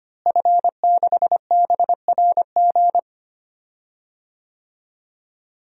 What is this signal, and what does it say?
Morse code, heard as a keyed tone, about 25 wpm, 700 Hz. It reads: F6BRG
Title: DXCC-100 Call Signs - 5 Characters - Repeated 2x - 25wpm